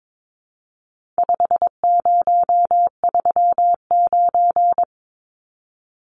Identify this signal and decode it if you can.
Morse code, heard as a keyed tone, about 22 wpm, 700 Hz. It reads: 5039
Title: Numbers - 4-Digit 22wpm